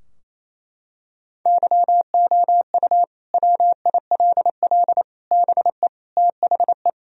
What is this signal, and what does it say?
Morse code, heard as a keyed tone, about 28 wpm, 700 Hz. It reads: YOU WILL BE THE
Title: Sentences from Top 1000 Words - Review Part 4 - 28wpm